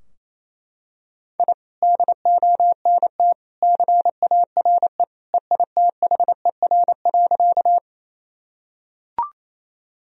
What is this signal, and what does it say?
Morse code, heard as a keyed tone, about 27 wpm, 700 Hz. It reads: I DONT CARE EITHER.
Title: Sentences from Top 1000 Words - Review Part 4 - 28wpm